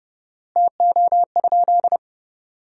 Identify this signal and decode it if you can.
Morse code, heard as a keyed tone, about 30 wpm, 700 Hz. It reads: TO?
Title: 4x Word-Spacing - Sentences from Top 100 Words - 30wpm